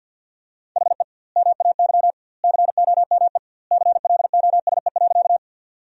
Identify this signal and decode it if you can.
Morse code, heard as a keyed tone, about 48 wpm, 700 Hz. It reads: HE MAY COME CLOSE.
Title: 2x Word-Spacing - Sentences from Top 300 Words - 50wpm